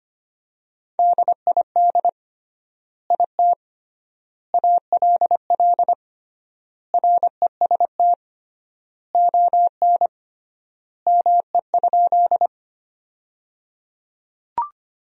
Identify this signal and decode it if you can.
Morse code, heard as a keyed tone, about 25 wpm, 700 Hz. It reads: DID IT ALL REST ON ME?
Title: 3x Word-Spacing - Sentences from Top 500 Words - 25wpm